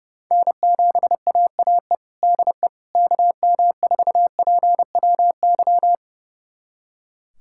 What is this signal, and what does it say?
Morse code, heard as a keyed tone, about 30 wpm, 700 Hz. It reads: N7AAE DE KM4PWY
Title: USA Callsign DE Callsign 30wpm